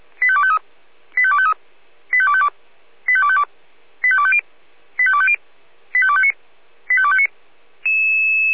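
Five-tone selective call (ZVEI), heard as quick runs of five tones, tones of about 70 ms. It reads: 85253 85253 85252 85252 85250 85250 85259 85259
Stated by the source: ZVEI five-tone selective call received by SDR on channel 496.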